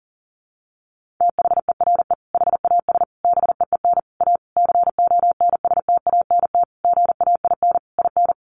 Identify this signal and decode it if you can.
Morse code, heard as a keyed tone, about 40 wpm, 700 Hz. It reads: THERE HAS BEEN A CONSTANT GAIN IN
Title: Sentences from Top 1500 Words 40wpm